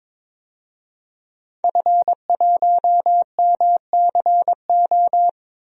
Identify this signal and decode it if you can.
Morse code, heard as a keyed tone, about 22 wpm, 700 Hz. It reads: F1MCO
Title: DXCC-100 Call Signs - 5 Characters - Repeated 2x - 22wpm